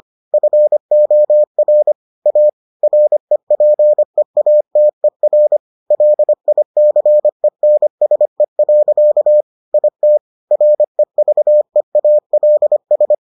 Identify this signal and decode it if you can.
Morse code, heard as a keyed tone, about 25 wpm, 600 Hz. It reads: FOR A REPEATER LICENSE. IT REVEALS